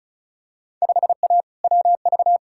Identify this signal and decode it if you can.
Morse code, heard as a keyed tone, about 36 wpm, 700 Hz. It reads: 5A WV
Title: Field Day Exchanges - 35wpm